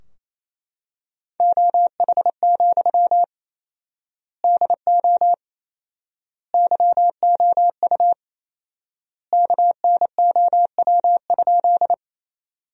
Morse code, heard as a keyed tone, about 28 wpm, 700 Hz. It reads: OH, DO YOU KNOW?
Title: 4x Word-Spacing - Sentences from Top 500 Words - 28wpm